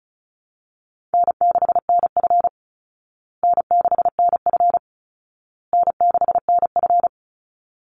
Morse code, heard as a keyed tone, about 34 wpm, 700 Hz. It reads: N6NF N6NF N6NF
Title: Call Signs Repeated 2x - 35wpm